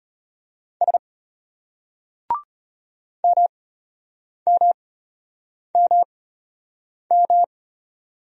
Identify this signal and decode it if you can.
Morse code, heard as a keyed tone, about 33 wpm, spaced as for 8 wpm, 700 Hz. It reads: S MMMM